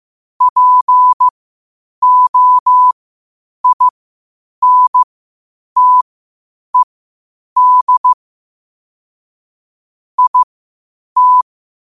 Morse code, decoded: POINTED IT

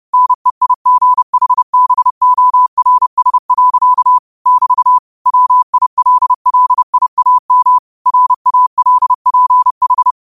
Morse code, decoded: NEIGHBORS. <BT> WILLIAM RALPH

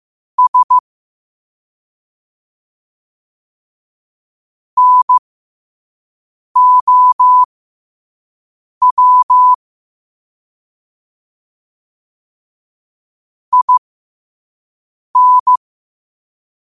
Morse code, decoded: S NOW IN

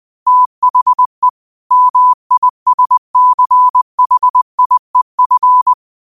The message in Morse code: THE MISCHIEF

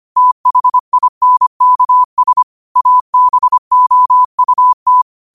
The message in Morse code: THINKS ABOUT

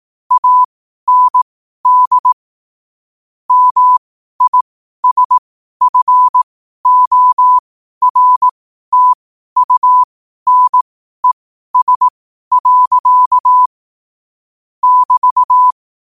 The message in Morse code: AND MISFORTUNES. <BT>

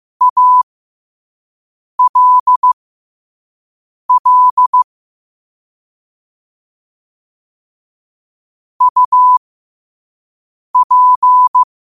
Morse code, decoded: ALL UP

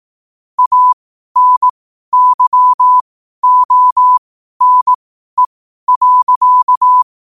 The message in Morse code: ANYONE.

